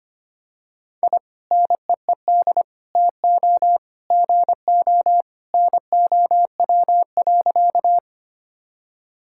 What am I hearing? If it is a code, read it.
Morse code, heard as a keyed tone, about 25 wpm, 700 Hz. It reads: I NEED TO GO NOW.
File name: practice / 25wpm / Sentences with 2-Word Phrases - 25wpm